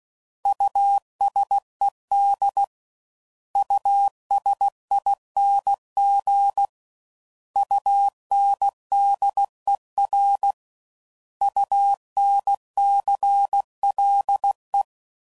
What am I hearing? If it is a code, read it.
Morse code, heard as a keyed tone, about 16 wpm, 800 Hz. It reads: USED USING UNDER UNCLE
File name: Prefixes / By First Letter / Us